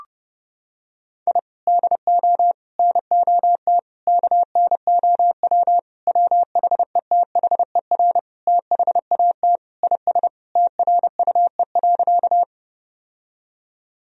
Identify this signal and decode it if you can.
Morse code, heard as a keyed tone, about 30 wpm, 700 Hz. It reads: I DO NOT KNOW WHETHER THAT IS TRUE.
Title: Sentences from Top 1000 Words - Review Part 3 - 30wpm